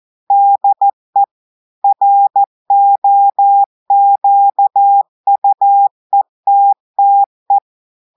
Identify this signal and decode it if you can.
Morse code, heard as a keyed tone, about 14 wpm, 800 Hz. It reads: DE ROQUETTE
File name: audios / portugais / portugais texte en clair 1 vitesse 14 mots minutes